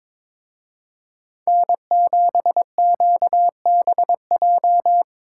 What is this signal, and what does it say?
Morse code, heard as a keyed tone, about 26 wpm, 700 Hz. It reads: N7QBJ
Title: Speed Racing - USA Call Signs 22wpm